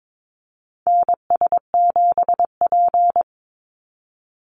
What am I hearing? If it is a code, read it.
Morse code, heard as a keyed tone, about 22 wpm, 700 Hz. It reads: NS7P
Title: NA SOTA Activator Callsigns 22wpm